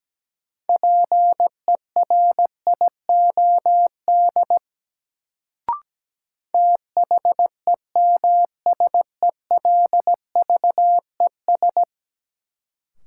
Morse code, heard as a keyed tone, about 17 wpm, 700 Hz. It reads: PERIOD THEMSELVES
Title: Sentences from Top 800 Words - New Word Drill - 17wpm